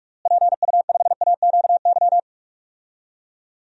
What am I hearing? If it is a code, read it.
Morse code, heard as a keyed tone, about 41 wpm, 700 Hz. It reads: PU5AQY
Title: DXCC-100 Call Signs - 6 Characters - Repeated 2x - 45wpm